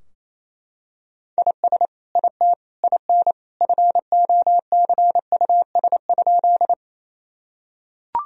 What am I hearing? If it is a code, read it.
Morse code, heard as a keyed tone, about 28 wpm, 700 Hz. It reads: IS IT IN FOCUS?
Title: Sentences from Top 1000 Words - Review Part 4 - 28wpm